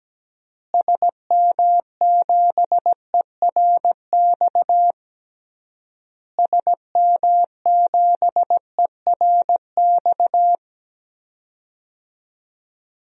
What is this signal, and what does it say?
Morse code, heard as a keyed tone, about 17 wpm, 700 Hz. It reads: SM7ERX SM7ERX
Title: DXCC-100 Call Signs - 6 Characters - Repeated 3x - 17wpm